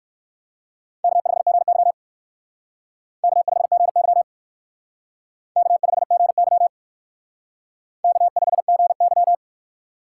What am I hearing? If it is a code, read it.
Morse code, heard as a keyed tone, about 56 wpm, 700 Hz. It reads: K5GY K5GY K5GY K5GY
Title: Speed Racing - USA 1x2 Call Signs 45wpm